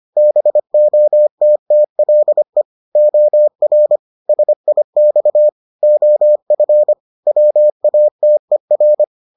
Morse code, heard as a keyed tone, about 25 wpm, 600 Hz. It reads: BOTTLE OR SIX OF WATER